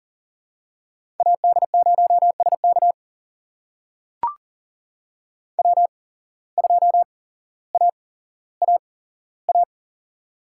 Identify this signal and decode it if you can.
Morse code, heard as a keyed tone, about 40 wpm, spaced as for 17 wpm, 700 Hz. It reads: AD0SK W2AAA
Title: USA Call Signs - 5 Characters - 8x Character Spacing - 40wpm